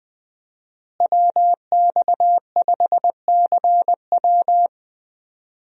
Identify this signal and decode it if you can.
Morse code, heard as a keyed tone, about 20 wpm, 700 Hz. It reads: WX5CW
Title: USA Call Signs - 5 Characters - Repeated 2x - 20wpm